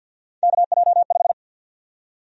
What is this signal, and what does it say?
Morse code, heard as a keyed tone, about 51 wpm, 700 Hz. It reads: KJ5
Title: USA Call Sign Prefixes - 50wpm